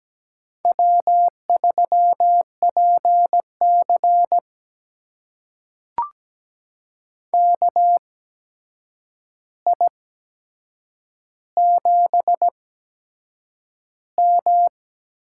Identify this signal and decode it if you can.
Morse code, heard as a keyed tone, about 17 wpm, spaced as for 7 wpm, 700 Hz. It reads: W3PC KI7M
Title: USA Call Signs - 8x Character Spacing - 17wpm